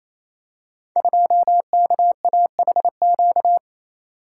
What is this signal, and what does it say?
Morse code, heard as a keyed tone, about 29 wpm, 700 Hz. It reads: 2KAHQ
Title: Code Groups - 5 Letter:Numbers - 28wpm